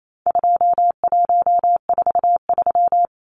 Morse code, heard as a keyed tone, about 28 wpm, 700 Hz. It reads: 2143
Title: Numbers - 4-Digit 28wpm